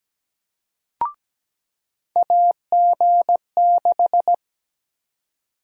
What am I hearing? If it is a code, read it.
Morse code, heard as a keyed tone, about 17 wpm, 700 Hz. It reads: AG6
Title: USA Call Sign Prefixes - 17wpm